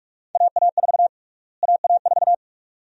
Morse code, heard as a keyed tone, about 46 wpm, 690 Hz. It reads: AA4 AA4
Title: USA Call Sign Prefixes - Repeated 2x - 45wpm